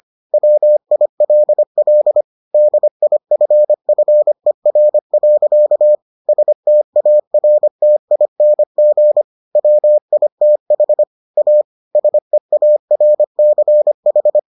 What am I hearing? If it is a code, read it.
Morse code, heard as a keyed tone, about 25 wpm, 600 Hz. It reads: WILL DIFFER. STARTING WITH A SEARCH